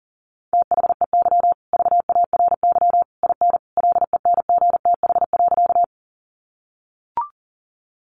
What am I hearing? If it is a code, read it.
Morse code, heard as a keyed tone, about 40 wpm, 700 Hz. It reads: THEY VARY IN LENGTH.
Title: Sentences from Top 1200 Words 40wpm